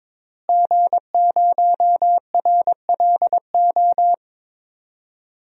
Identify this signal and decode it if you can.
Morse code, heard as a keyed tone, about 22 wpm, 700 Hz. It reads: G0RLO
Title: DXCC-100 Call Signs - 5 Characters - Repeated 3x - 22wpm